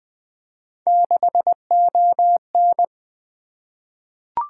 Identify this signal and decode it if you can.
Morse code, heard as a keyed tone, about 20 wpm, 700 Hz. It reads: 6ON
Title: Three Letter-Number Combinations - 20wpm